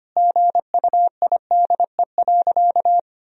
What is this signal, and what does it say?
Morse code, heard as a keyed tone, about 25 wpm, 700 Hz. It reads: GUIDE.
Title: Sentences from Top 1000 Words - Review Part 3 - 25wpm